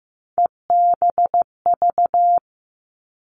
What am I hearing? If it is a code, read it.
Morse code, heard as a keyed tone, about 15 wpm, 700 Hz. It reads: EBV
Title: Three Challenging Letter-Number Combinations 15wpm